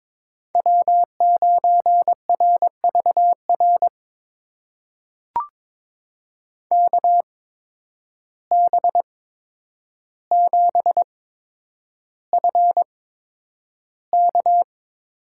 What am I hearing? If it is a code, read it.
Morse code, heard as a keyed tone, about 22 wpm, spaced as for 9 wpm, 700 Hz. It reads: W9RVR KB7FK
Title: USA Call Signs - 5 Characters - 8x Character Spacing - 22wpm